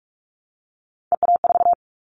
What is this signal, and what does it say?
Morse code, heard as a keyed tone, about 44 wpm, 700 Hz. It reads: EA4
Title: Callsign Prefixes for Top 25 DXCC - Repeated 3x - 45wpm